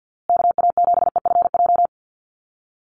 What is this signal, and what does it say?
Morse code, heard as a keyed tone, about 50 wpm, 700 Hz. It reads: KA7EFJ
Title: USA Call Signs Repeated 2x 50wpm